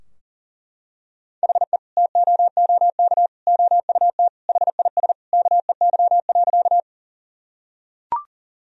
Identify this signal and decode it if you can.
Morse code, heard as a keyed tone, about 40 wpm, 700 Hz. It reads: HE TOOK OUT HIS KEY.
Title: Sentences from Top 1000 Words - Review Part 4 - 40wpm